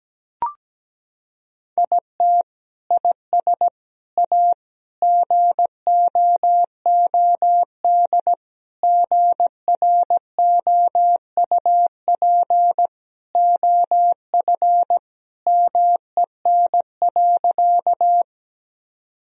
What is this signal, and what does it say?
Morse code, heard as a keyed tone, about 17 wpm, 700 Hz. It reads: IT IS A GOOD GROUP OF MEN.
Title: Sentences from Top 1000 Words - Review Part 1 - 17wpm